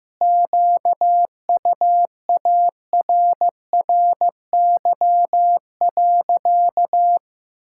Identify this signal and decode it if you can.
Morse code, heard as a keyed tone, about 15 wpm, 700 Hz. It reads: QUARRY.
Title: Pangram Challenge v2 - 15wpm